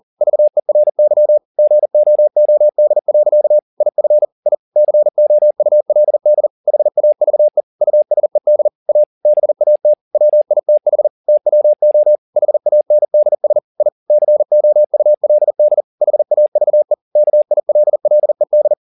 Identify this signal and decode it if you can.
Morse code, heard as a keyed tone, about 40 wpm, 600 Hz. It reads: VERY GOOD. IF I COULD HAVE USED A BAT WITH TWO HANDS I COULD HAVE KILLED
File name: the old man and the sea 2 40wpm